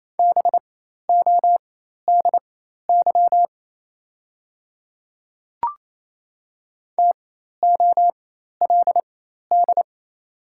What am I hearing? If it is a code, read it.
Morse code, heard as a keyed tone, about 28 wpm, spaced as for 13 wpm, 700 Hz. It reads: BODY TOLD
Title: 4x Character Spacing - Top 500 Words - 28wpm